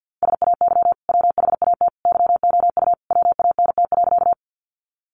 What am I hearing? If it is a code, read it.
Morse code, heard as a keyed tone, about 50 wpm, 700 Hz. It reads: SAY WHAT YOU WANT.
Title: Sentences from Top 200 Words 50wpm